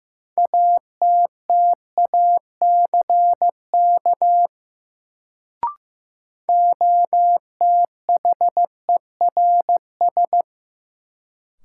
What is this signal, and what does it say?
Morse code, heard as a keyed tone, about 15 wpm, 700 Hz. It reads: ATTACK OTHERS
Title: Sentences from Top 1500 Words - New Word Drill Alt - 15wpm